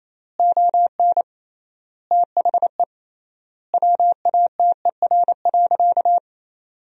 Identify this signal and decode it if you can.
Morse code, heard as a keyed tone, about 28 wpm, 700 Hz. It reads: ON THE WATER.
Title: 3x Word-Spacing - Sentences from Top 500 Words - 28wpm